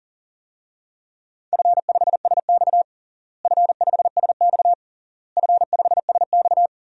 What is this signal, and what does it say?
Morse code, heard as a keyed tone, about 37 wpm, 700 Hz. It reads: F5SX F5SX F5SX
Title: DXCC-100 Call Signs - 4 Characters - Repeated 3x - 40wpm